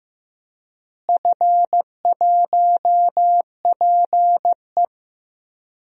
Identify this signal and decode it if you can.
Morse code, heard as a keyed tone, about 15 wpm, 700 Hz. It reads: F1PE
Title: DXCC-100 Call Signs - 4 Characters - Repeated 2x - 15wpm